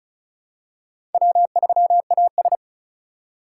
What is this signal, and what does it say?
Morse code, heard as a keyed tone, about 36 wpm, 700 Hz. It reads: W3AS
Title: USA Call Signs - 2x Character Spacing - 35wpm